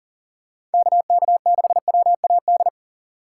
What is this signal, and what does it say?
Morse code, heard as a keyed tone, about 37 wpm, 700 Hz. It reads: KK6WAD
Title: USA Call Signs - 6 Characters - 4x Character Spacing - 40wpm